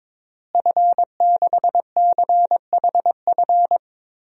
Morse code, heard as a keyed tone, about 21 wpm, 700 Hz. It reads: F6CHF
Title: DXCC-100 Call Signs - 5 Characters - Repeated 3x - 22wpm